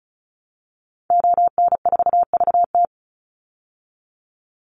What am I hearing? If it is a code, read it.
Morse code, heard as a keyed tone, about 34 wpm, 700 Hz. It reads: ON4VT
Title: Call Signs Repeated 3x - 35wpm